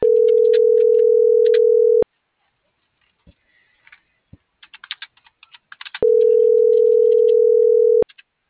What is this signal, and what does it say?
Ringback tone (North America), 440 + 480 Hz.
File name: CENTRAL MULTIMODAL / out rodolfo.fuentes